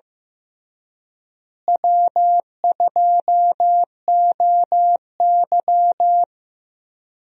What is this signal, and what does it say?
Morse code, heard as a keyed tone, about 15 wpm, 700 Hz. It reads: W2OY
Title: USA Call Signs - 4 Characters - Repeated 3x - 15wpm